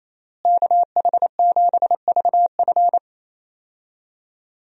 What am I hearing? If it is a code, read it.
Morse code, heard as a keyed tone, about 28 wpm, 700 Hz. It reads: KH7VF
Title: USA Call Signs - 5 Characters - Repeated 2x - 28wpm